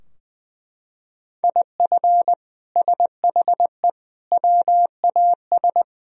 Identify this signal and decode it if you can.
Morse code, heard as a keyed tone, about 20 wpm, 700 Hz. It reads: IF SHE WAS
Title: Sentences from Top 1000 Words - Review Part 2 - 20wpm